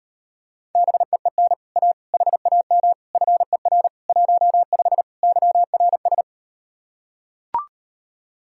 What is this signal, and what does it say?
Morse code, heard as a keyed tone, about 38 wpm, 700 Hz. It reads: BEEN A HAM FER 15 YRS